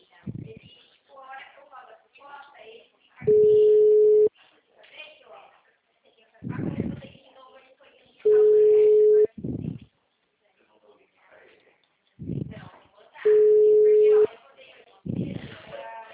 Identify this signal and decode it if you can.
Ringback tone (Europe, 425 Hz), with speech.